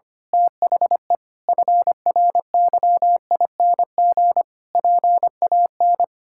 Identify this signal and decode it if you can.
Morse code, heard as a keyed tone, about 25 wpm, 700 Hz. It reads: THE FRYING PAN
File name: practice / 25wpm / Sentences with 9-Word Phrases - 25wpm